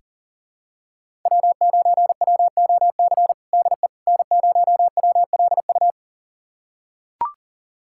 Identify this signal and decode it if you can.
Morse code, heard as a keyed tone, about 40 wpm, 700 Hz. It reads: W9WOC DE N0WLU
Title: USA Callsign DE Callsign - 5 Characters - 40wpm